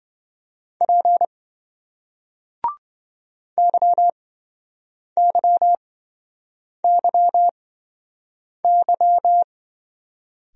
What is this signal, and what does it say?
Morse code, heard as a keyed tone, about 28 wpm, spaced as for 7 wpm, 700 Hz. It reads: P YYYY